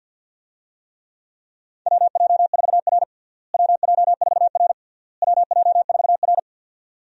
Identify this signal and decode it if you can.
Morse code, heard as a keyed tone, about 52 wpm, 700 Hz. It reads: WJ4R WJ4R WJ4R
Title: USA Call Signs - 4 Characters - Repeated 3x - 50wpm